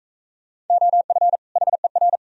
Morse code, heard as a keyed tone, about 40 wpm, 700 Hz. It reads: OF HER